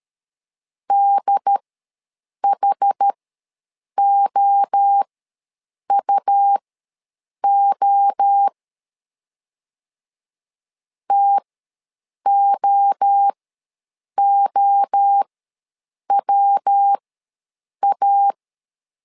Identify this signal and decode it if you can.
Morse code, heard as a keyed tone, about 13 wpm, spaced as for 7 wpm, 800 Hz. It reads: DHOUO TOOWA
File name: LEZIONI / 3 MWRO / 3 MWRO ws 12wpm 1914